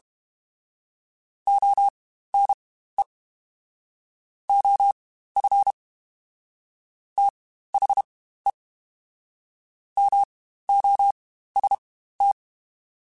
Morse code, heard as a keyed tone, about 32 wpm, spaced as for 15 wpm, 800 Hz. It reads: ONE OF THE MOST
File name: Quotes Spaced / Quotes Spaced-mp3s / ChildAsleep sp @30